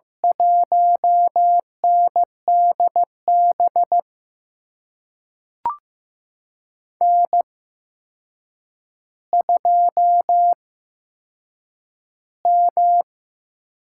Morse code, heard as a keyed tone, about 15 wpm, spaced as for 7 wpm, 700 Hz. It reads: 1NDB N2M
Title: USA Call Signs - 5 Characters - 8x Character Spacing - 15wpm